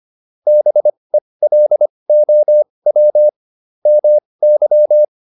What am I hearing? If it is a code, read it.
Morse code, heard as a keyed tone, about 25 wpm, spaced as for 20 wpm, 600 Hz. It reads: BELOW MY